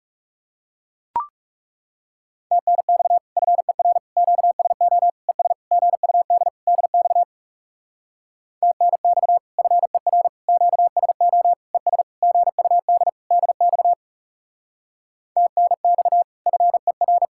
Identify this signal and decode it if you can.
Morse code, heard as a keyed tone, about 39 wpm, 700 Hz. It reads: TNX FER QSO ES GUD DX TNX FER QSO ES GUD DX TNX FER